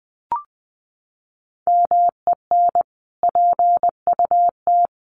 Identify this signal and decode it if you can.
Morse code, heard as a keyed tone, about 19 wpm, 700 Hz. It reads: MEN PUT
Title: Sentences from Top 1400 Words 20wpm